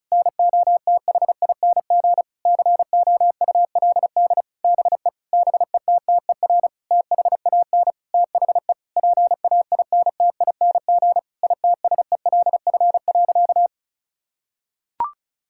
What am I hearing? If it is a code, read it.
Morse code, heard as a keyed tone, about 36 wpm, 700 Hz. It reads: NOTHING COULD BE BETTER THAN THE PAINTING ITSELF.
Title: Sentences from Top 1000 Words - Review Part 6 - 35wpm